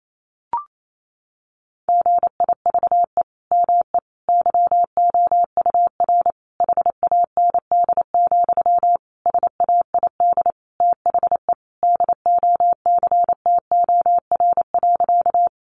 Morse code, heard as a keyed tone, about 28 wpm, 700 Hz. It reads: GIVE ME YOUR HAND, SAID THE DOCTOR.
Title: Sentences from Top 1500 Words 28wpm